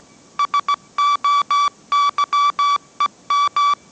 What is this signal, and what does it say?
Morse code, heard as a keyed tone, about 17 wpm, 1150 Hz. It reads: SOYEM
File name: mrx / mrx080 18wpm